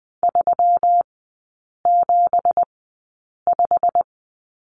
Morse code, heard as a keyed tone, about 20 wpm, spaced as for 8 wpm, 700 Hz. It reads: 375
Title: 212 Lesson 34e - Number 8 - Three Characters 20wpm